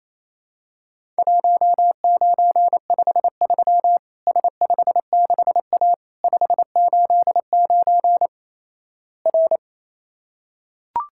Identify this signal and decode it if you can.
Morse code, heard as a keyed tone, about 28 wpm, 700 Hz. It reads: 1953 S56A 589 R
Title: Worked All Europe DX Contest - QTC Exchange - 28wpm